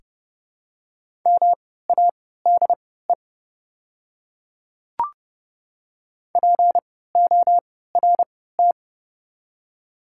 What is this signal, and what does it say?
Morse code, heard as a keyed tone, about 30 wpm, spaced as for 17 wpm, 700 Hz. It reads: MADE PORT
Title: 3x Character Spacing - Top 500 Words - 30wpm